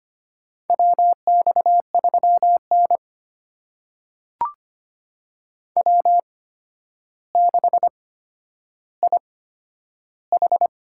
Morse code, heard as a keyed tone, about 25 wpm, spaced as for 11 wpm, 700 Hz. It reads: WX3N W6IH
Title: USA Call Signs - 4 Characters - 8x Character Spacing - 25wpm